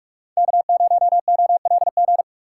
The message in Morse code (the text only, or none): K0OLG